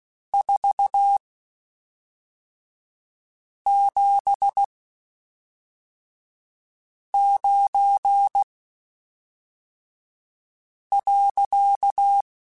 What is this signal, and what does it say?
Morse code, heard as a keyed tone, about 16 wpm, spaced as for 3 wpm, 800 Hz. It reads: 479.
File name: (spaced) / QSOs Spaced-mp3s / kc7sac @5 sp